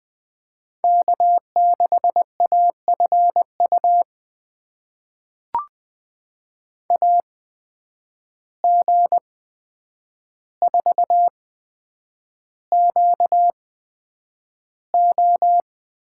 Morse code, heard as a keyed tone, about 20 wpm, spaced as for 9 wpm, 700 Hz. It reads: K6AFU AG4QO
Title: USA Call Signs - 5 Characters - 8x Character Spacing - 20wpm